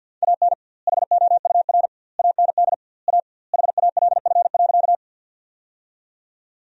Morse code, heard as a keyed tone, about 48 wpm, 700 Hz. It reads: AN HOUR AND A HALF.
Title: 2x Word-Spacing - Sentences from Top 400 Words - 50wpm